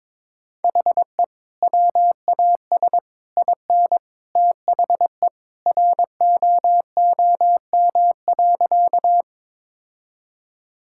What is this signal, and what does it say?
Morse code, heard as a keyed tone, about 22 wpm, 700 Hz. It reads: HE WAS IN THE ROOM.
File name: practice / 22wpm / Sentences with 2-Word Phrases - 22wpm